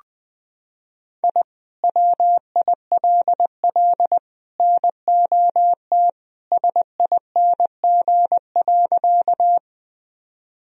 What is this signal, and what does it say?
Morse code, heard as a keyed tone, about 21 wpm, 700 Hz. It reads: I WILL NOT SING.
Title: Sentences from Top 1000 Words - Review Part 2 - 20wpm